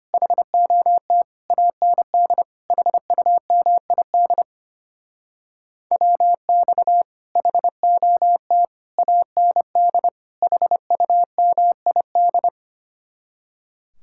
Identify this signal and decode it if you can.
Morse code, heard as a keyed tone, about 33 wpm, 700 Hz. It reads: HOT AND HUMID WX HOT AND HUMID